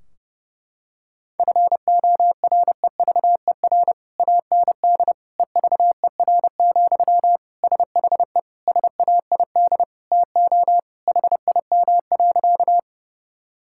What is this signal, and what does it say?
Morse code, heard as a keyed tone, about 28 wpm, 700 Hz. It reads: FOREVER AND EVER, SHE SAID TO HIM.
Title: Sentences from Top 1600 Words 30wpm